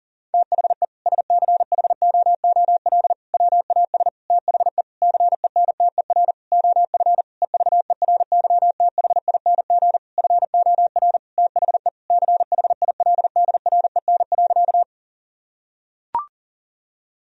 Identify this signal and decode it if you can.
Morse code, heard as a keyed tone, about 40 wpm, 700 Hz. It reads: THE SCHOOL WAS THE CENTER OF EVERYTHING FOR THE CHILDREN.
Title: Sentences from Top 1000 Words - Review Part 2 - 40wpm